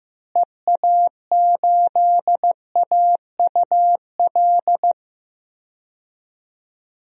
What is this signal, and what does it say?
Morse code, heard as a keyed tone, about 15 wpm, 700 Hz. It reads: EA8AUL
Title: DXCC-100 Call Signs - 6 Characters - Repeated 3x - 15wpm